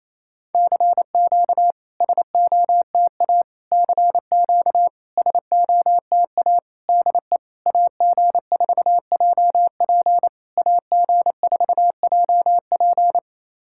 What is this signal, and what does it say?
Morse code, heard as a keyed tone, about 28 wpm, 690 Hz. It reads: CQ SOTA CQ SOTA DE AG4JP AG4JP
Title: SOTA QSO Practice 28wpm